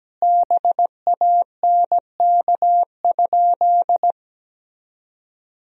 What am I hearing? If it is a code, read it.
Morse code, heard as a keyed tone, about 17 wpm, 700 Hz. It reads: BANK?
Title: Sentences from Top 1000 Words - Review Part 4 - 17wpm